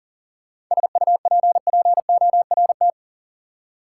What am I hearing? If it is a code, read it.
Morse code, heard as a keyed tone, about 40 wpm, 700 Hz. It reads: SUPPORT
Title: Seven Letter Words - 40wpm